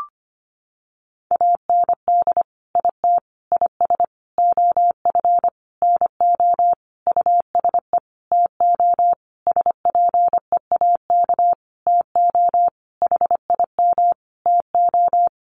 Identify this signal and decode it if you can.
Morse code, heard as a keyed tone, about 25 wpm, 700 Hz. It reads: AND IT IS OF NO USE TO SPEAK TO HIM TO
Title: Sentences from Top 1100 Words 25wpm